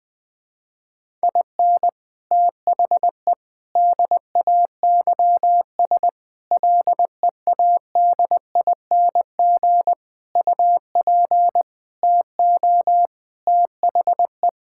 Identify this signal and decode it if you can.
Morse code, heard as a keyed tone, about 21 wpm, 700 Hz. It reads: IN THE DAYS LEADING UP TO THE
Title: Sentences with 6-Word Phrases - 20wpm